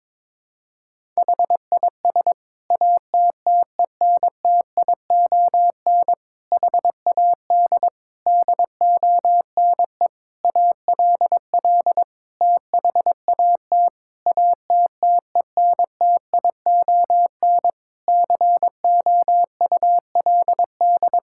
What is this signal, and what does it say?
Morse code, heard as a keyed tone, about 21 wpm, 700 Hz. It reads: HIS ATTENTION HAD DONE ALL THAT ATTENTION COULD
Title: Sentences from Top 1000 Words - Review Part 6 - 22wpm